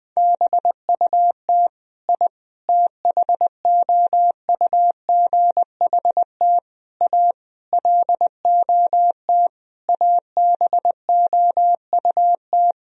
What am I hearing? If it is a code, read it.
Morse code, heard as a keyed tone, about 20 wpm, 700 Hz. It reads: BUT I THOUGHT A LOT ABOUT
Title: Sentences from Top 1000 Words - Review Part 3 - 20wpm